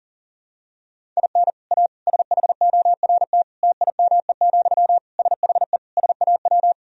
Morse code, heard as a keyed tone, about 40 wpm, 700 Hz. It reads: IN A SHORT TIME, SHE SAW
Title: Sentences from Top 1000 Words - Review Part 2 - 40wpm